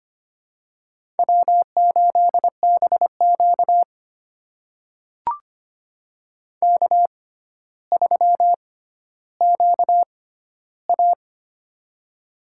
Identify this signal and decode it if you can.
Morse code, heard as a keyed tone, about 25 wpm, spaced as for 13 wpm, 700 Hz. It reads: W8BQ K3QA
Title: USA Call Signs - 4 Characters - 6x Character Spacing - 25wpm